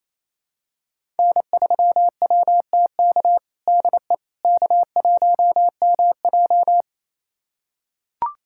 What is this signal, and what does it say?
Morse code, heard as a keyed tone, about 28 wpm, 700 Hz. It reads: N3WTK DE K1MJ
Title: USA Callsign DE Callsign 28wpm